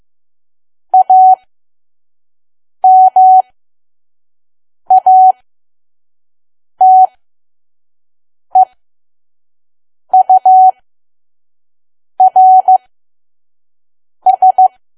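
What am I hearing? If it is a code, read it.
Morse code, heard as a keyed tone, about 15 wpm, spaced as for 5 wpm, 750 Hz. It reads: AMATEURS
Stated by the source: Morse code practice at 5 WPM from KQ2P.COM.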